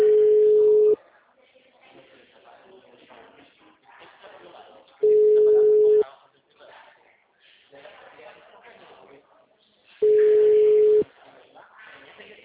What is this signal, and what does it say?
Ringback tone (Europe, 425 Hz), with speech.